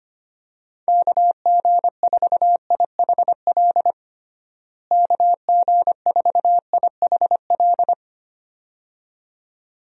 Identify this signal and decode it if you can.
Morse code, heard as a keyed tone, about 25 wpm, 700 Hz. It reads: KG4IHL KG4IHL
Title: USA Call Signs - 6 Characters - Repeated 2x - 25wpm